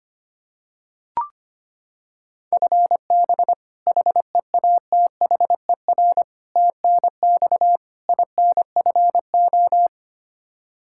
Morse code, heard as a keyed tone, about 25 wpm, 700 Hz. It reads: FB HEATHER TNX INFO
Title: Practice QSO - Exchange 3 - More Info - 25wpm